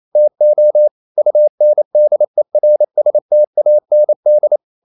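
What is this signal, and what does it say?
Morse code, heard as a keyed tone, about 28 wpm, 600 Hz. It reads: TO UNDERSTAND
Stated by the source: Morse code practice at 28 wpm, text of Ken Follett's novel Never in English, from CW med Gnister.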